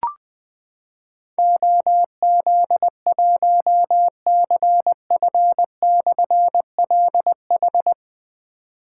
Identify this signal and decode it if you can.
Morse code, heard as a keyed tone, about 20 wpm, 700 Hz. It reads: OZ1CF/LH
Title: Difficult Call Signs 20wpm